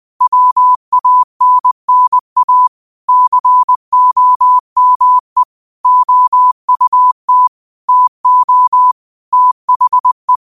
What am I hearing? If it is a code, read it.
Morse code, heard as a keyed tone, about 20 wpm, 1000 Hz. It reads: WANNA COME OUT TO THE
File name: qotdfiles / 20wpm